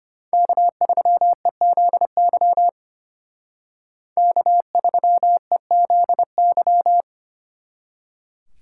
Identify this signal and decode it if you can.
Morse code, heard as a keyed tone, about 31 wpm, 700 Hz. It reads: K3EZY K3EZY
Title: Speed Racing - USA 1x3 Call Signs 25wpm